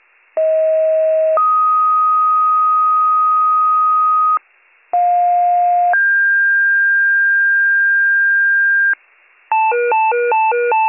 Two-tone page, heard as a one-second tone, then a three-second tone, about 640 then 1250 Hz, about 710 then 1600 Hz, with noise. It is heard from 0.4 to 8.9 s.